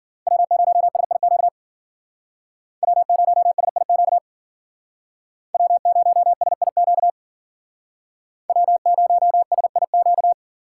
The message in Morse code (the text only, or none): W0SIQ W0SIQ W0SIQ W0SIQ